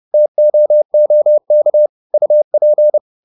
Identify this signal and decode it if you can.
Morse code, heard as a keyed tone, about 30 wpm, 600 Hz. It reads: TOOK UP